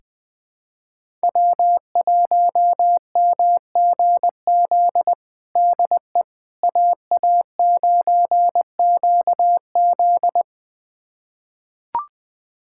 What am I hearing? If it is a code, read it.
Morse code, heard as a keyed tone, about 20 wpm, 700 Hz. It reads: W1MGZ DE AA9QZ
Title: USA Callsign DE Callsign - 5 Characters - 20wpm